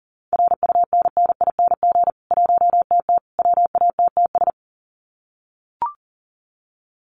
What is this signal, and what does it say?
Morse code, heard as a keyed tone, about 40 wpm, 700 Hz. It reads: RUNNING 1TT WATTS
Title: Common QSO Phrases 40wpm